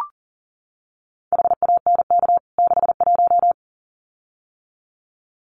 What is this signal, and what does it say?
Morse code, heard as a keyed tone, about 40 wpm, 700 Hz. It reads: HANK 61
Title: CWT Exchange 40wpm